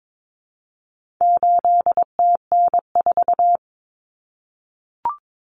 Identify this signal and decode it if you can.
Morse code, heard as a keyed tone, about 22 wpm, 700 Hz. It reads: 8TN4
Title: Common Cut Numbers - 4-Digit 22wpm